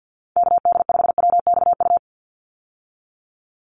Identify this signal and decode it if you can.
Morse code, heard as a keyed tone, about 50 wpm, 700 Hz. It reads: KD5WXU
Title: USA Call Signs 50wpm